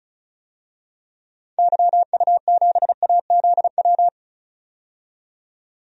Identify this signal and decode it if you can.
Morse code, heard as a keyed tone, about 34 wpm, 700 Hz. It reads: YU7AZW
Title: DXCC-100 Call Signs - 6 Characters - 35wpm